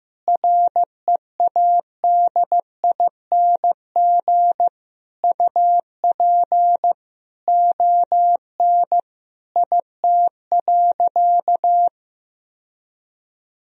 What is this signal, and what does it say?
Morse code, heard as a keyed tone, about 15 wpm, 700 Hz. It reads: READING UP ON IT.
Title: Sentences from Top 1000 Words - Review Part 3 - 15wpm